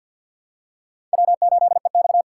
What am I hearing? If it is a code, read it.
Morse code, heard as a keyed tone, about 46 wpm, 700 Hz. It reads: W8EX
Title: USA Call Signs - 4 Characters - 6x Character Spacing - 50wpm